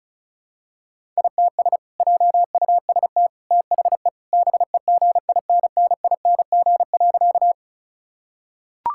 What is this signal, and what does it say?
Morse code, heard as a keyed tone, about 36 wpm, 700 Hz. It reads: ITS JUST THE BEGINNING.
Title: Sentences from Top 1000 Words - Review Part 5 - 35wpm